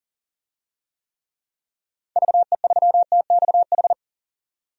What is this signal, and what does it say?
Morse code, heard as a keyed tone, about 37 wpm, 700 Hz. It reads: VE3TXH
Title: DXCC-100 Call Signs - 6 Characters - 40wpm